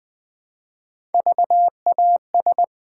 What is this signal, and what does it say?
Morse code, heard as a keyed tone, about 20 wpm, 700 Hz. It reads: VAS
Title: Build A Word - 1 Letter At A Time - 4-Letter Words - 20wpm